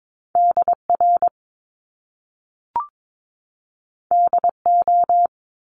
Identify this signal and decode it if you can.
Morse code, heard as a keyed tone, about 22 wpm, 700 Hz. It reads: DR DO
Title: Top Two Letter Combinations in Words 22wpm